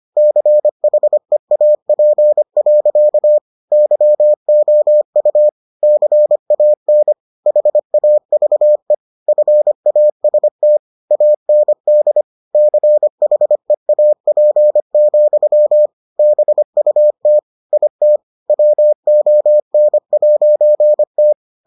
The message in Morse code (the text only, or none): CHEAP. YOU CAN HAVE FAST AND CHEAP, BUT IT WON'T